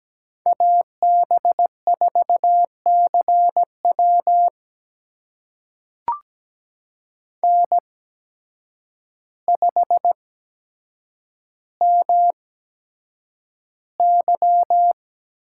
Morse code, heard as a keyed tone, about 17 wpm, spaced as for 8 wpm, 700 Hz. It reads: AB4CW N5MY
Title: USA Call Signs - 8x Character Spacing - 17wpm